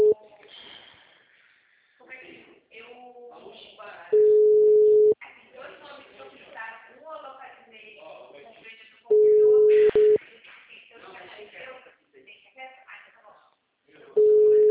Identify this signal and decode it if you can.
Ringback tone (Europe, 425 Hz), with speech.
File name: out- -2044- - - .38